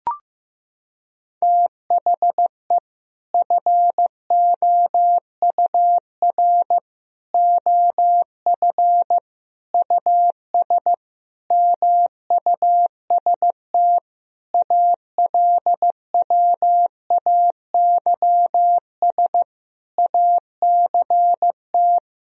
Morse code, heard as a keyed tone, about 16 wpm, 700 Hz. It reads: THE FOUR OF US MUST ALWAYS ACT
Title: Sentences from Top 1000 Words - Review Part 1 - 15wpm